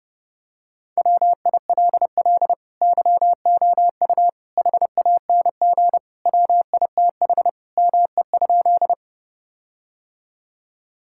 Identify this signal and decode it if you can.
Morse code, heard as a keyed tone, about 30 wpm, 700 Hz. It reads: WILL YOU HANG WITH ME?
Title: Sentences from Top 1600 Words 30wpm